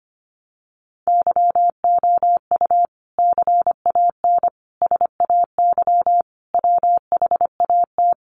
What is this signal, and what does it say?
Morse code, heard as a keyed tone, about 25 wpm, 700 Hz. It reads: YOU CAN SAY WHAT
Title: Sentences from Top 200 Words 25wpm